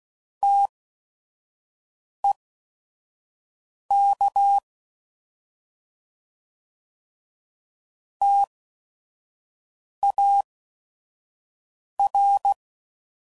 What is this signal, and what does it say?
Morse code, heard as a keyed tone, about 16 wpm, spaced as for 5 wpm, 800 Hz. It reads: TEK TAR